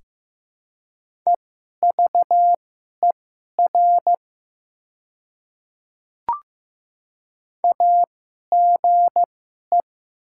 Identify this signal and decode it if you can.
Morse code, heard as a keyed tone, about 15 wpm, spaced as for 11 wpm, 700 Hz. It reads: EVER AGE
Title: 2x Character Spacing - Top 500 Words - 15wpm